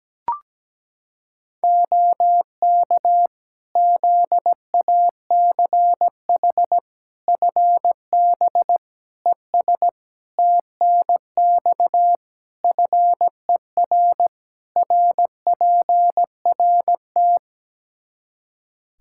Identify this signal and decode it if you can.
Morse code, heard as a keyed tone, about 17 wpm, 700 Hz. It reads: OK ZACH FB ES TNX FER RPRT
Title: K1USN SST QSO Practice 17wpm